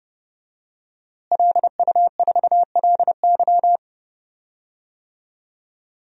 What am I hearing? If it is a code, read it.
Morse code, heard as a keyed tone, about 30 wpm, 700 Hz. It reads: LU4LY
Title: DXCC-100 Call Signs - 5 Characters - Repeated 3x - 30wpm